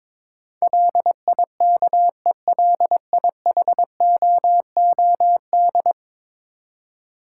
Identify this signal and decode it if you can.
Morse code, heard as a keyed tone, about 22 wpm, 700 Hz. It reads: LIKELIHOOD
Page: morse-code-ninja-practice-22wpm